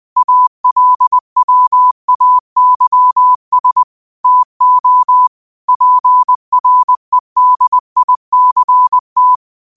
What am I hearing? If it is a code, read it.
Morse code, heard as a keyed tone, about 20 wpm, 1000 Hz. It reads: ALWAYS TO PREDICT
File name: qotdfiles / 20wpm